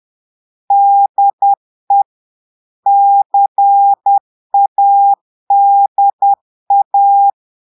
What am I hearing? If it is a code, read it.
Morse code, heard as a keyed tone, about 10 wpm, 800 Hz. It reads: DE CADA
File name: Portugais - texte en clair 1 - vitesse - 10 mots minute